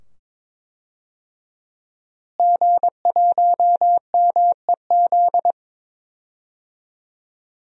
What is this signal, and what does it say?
Morse code, heard as a keyed tone, about 22 wpm, 700 Hz. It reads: G1MEZ
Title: DXCC-100 Call Signs - 5 Characters - Repeated 2x - 22wpm